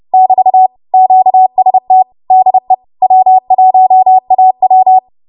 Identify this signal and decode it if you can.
Morse code, heard as a keyed tone, about 30 wpm, 750 Hz. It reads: <BT> QST DE W1AW
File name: Morse / 30WPM